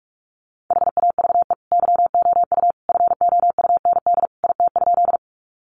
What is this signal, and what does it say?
Morse code, heard as a keyed tone, about 44 wpm, 700 Hz. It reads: HAVE YOU FOUND IT?
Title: Sentences from Top 300 Words 45wpm